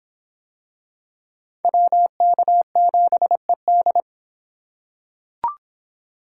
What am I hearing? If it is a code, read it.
Morse code, heard as a keyed tone, about 25 wpm, 700 Hz. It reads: WK7ED